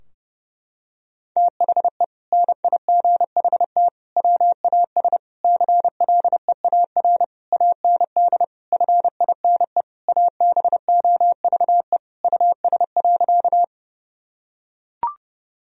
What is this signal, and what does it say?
Morse code, heard as a keyed tone, about 32 wpm, 700 Hz. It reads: THE NIGHT WAS CLEAR AND FINE ABOVE US.
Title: Sentences from Top 1000 Words - Review Part 2 - 30wpm